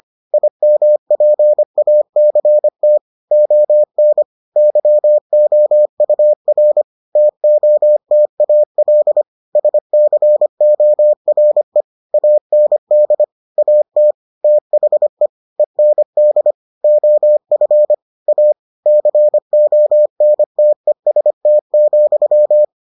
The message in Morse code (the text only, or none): IMPACT ON YOUR TOTAL SCORE AND AT THE END OF A CONTEST,